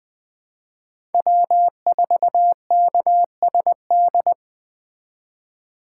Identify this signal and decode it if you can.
Morse code, heard as a keyed tone, about 21 wpm, 700 Hz. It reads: W4KSD